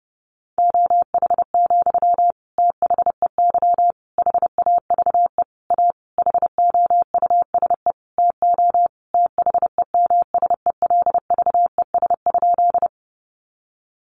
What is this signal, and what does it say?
Morse code, heard as a keyed tone, about 30 wpm, 700 Hz. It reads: OH, THEY HAVE A HOUSE TO THEMSELVES?
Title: Sentences from Top 800 Words 30wpm